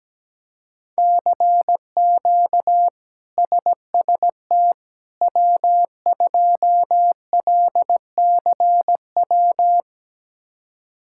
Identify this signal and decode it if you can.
Morse code, heard as a keyed tone, about 17 wpm, 700 Hz. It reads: CQ SST W2LCW
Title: K1USN SST QSO Practice - 17wpm